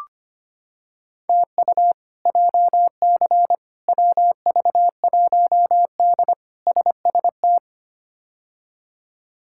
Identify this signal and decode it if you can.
Morse code, heard as a keyed tone, about 25 wpm, 700 Hz. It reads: TU JC WV1D SST
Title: K1USN SST QSO Practice - 25wpm